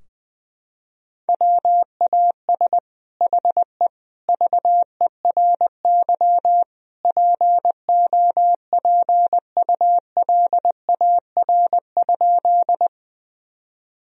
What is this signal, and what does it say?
Morse code, heard as a keyed tone, about 21 wpm, 700 Hz. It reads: WAS HE VERY POPULAR?
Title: Sentences from Top 1000 Words - Review Part 3 - 20wpm